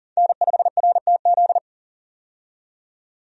Morse code, heard as a keyed tone, about 38 wpm, 700 Hz. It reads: N5RTZ
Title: USA Call Signs - 5 Characters - 40wpm